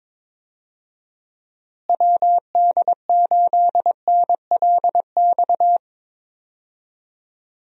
Morse code, heard as a keyed tone, about 22 wpm, 700 Hz. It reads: WD8NLX
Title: USA Call Signs - 6 Characters - Repeated 2x - 22wpm